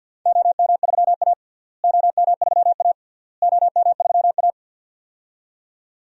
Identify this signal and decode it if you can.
Morse code, heard as a keyed tone, about 52 wpm, 700 Hz. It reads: OM3A OM3A OM3A
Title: DXCC-100 Call Signs - 4 Characters - Repeated 3x - 50wpm